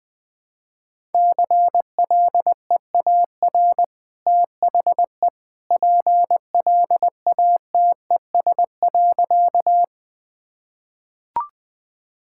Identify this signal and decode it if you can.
Morse code, heard as a keyed tone, about 21 wpm, 700 Hz. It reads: CLEAR THE PLATES.
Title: Easy Sentences with Hints - 20wpm